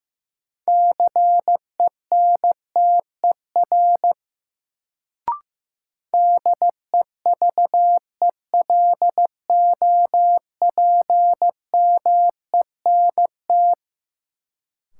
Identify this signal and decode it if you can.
Morse code, heard as a keyed tone, about 15 wpm, 700 Hz. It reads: CENTER DEVELOPMENT
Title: Sets of 3 Words - New Word Drill - 15wpm